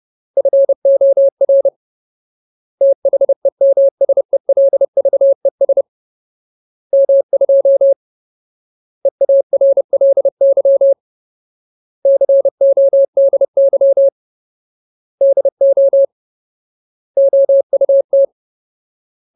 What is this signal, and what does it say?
Morse code, heard as a keyed tone, about 30 wpm, 550 Hz. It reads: FOR THEMSELVES M2 EARLY CODY DO OUT
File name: Ultimate Word List / 30WPM Ultimate